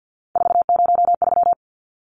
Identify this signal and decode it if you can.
Morse code, heard as a keyed tone, about 50 wpm, 700 Hz. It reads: 403
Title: Numbers - 3-Digit 50wpm